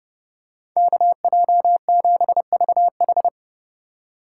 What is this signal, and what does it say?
Morse code, heard as a keyed tone, about 32 wpm, 700 Hz. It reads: KJ7VH